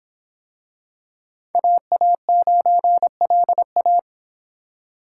Morse code, heard as a keyed tone, about 24 wpm, 700 Hz. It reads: AA9LA